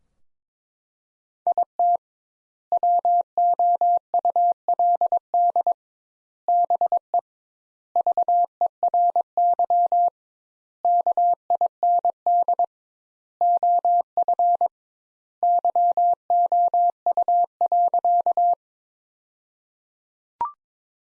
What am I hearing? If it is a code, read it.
Morse code, heard as a keyed tone, about 21 wpm, 700 Hz. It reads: IT WOULD BE VERY KIND OF YOU.
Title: Sentences Top 200 Words - 22wpm